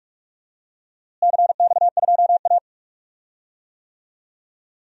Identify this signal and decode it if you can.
Morse code, heard as a keyed tone, about 46 wpm, 700 Hz. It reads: CX2A
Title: DXCC-100 Call Signs - 4 Characters - Repeated 3x - 45wpm